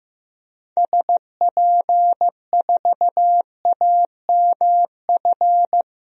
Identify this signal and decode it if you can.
Morse code, heard as a keyed tone, about 15 wpm, 700 Hz. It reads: SP4AMF
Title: DXCC-100 Call Signs - 6 Characters - Repeated 2x - 15wpm